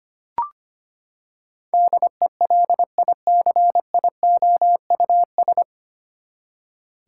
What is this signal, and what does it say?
Morse code, heard as a keyed tone, about 25 wpm, 700 Hz. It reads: DELICIOUS
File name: Nine Letter Words - 25wpm